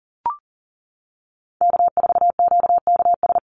Morse code, heard as a keyed tone, about 40 wpm, 700 Hz. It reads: K4QKS
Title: USA Call Signs 40wpm